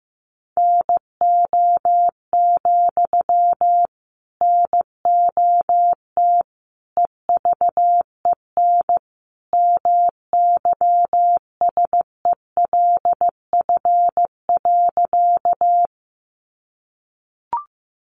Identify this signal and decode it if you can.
Morse code, heard as a keyed tone, about 15 wpm, 700 Hz. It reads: NO, NOT EVEN MYSELF.
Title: Sentences from Top 1300 Words 15wpm